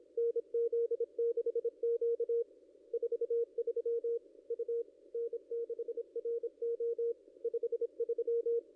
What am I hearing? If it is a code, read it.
Morse code, heard as a keyed tone, about 27 wpm, 470 Hz. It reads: NZ6Q 43 U N6RO 53
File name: contests / ss-cw-k5tr-op-n5rz -0800